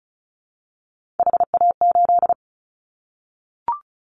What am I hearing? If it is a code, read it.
Morse code, heard as a keyed tone, about 34 wpm, 700 Hz. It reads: HA8
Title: Callsign Prefixes for Top 25 DXCC - 35wpm